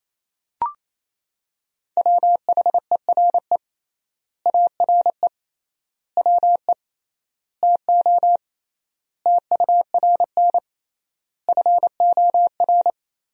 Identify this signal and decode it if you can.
Morse code, heard as a keyed tone, about 28 wpm, 700 Hz. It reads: WHERE ARE WE TO TURN FOR
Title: 3x Word-Spacing - Sentences from Top 200 Words - 28wpm